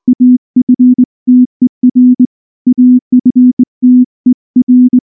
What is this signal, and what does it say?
Morse code, heard as a keyed tone, about 20 wpm, 250 Hz. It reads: AFTER AFTER
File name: morse / morse-training-20WPM-2